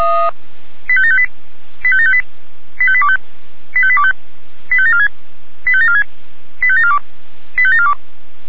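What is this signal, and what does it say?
Five-tone selective call (ZVEI), heard as quick runs of five tones, tones of about 70 ms. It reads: 85758 85758 85825 85825 85746 85746 85742 85742
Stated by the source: Five-tone ZVEI selective call 85827 received by SDR.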